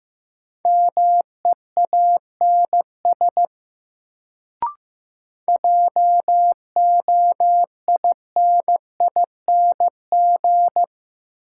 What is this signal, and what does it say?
Morse code, heard as a keyed tone, about 16 wpm, 700 Hz. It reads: MEANS JOINING
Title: Sets of 5 Words - New Word Drill - 15wpm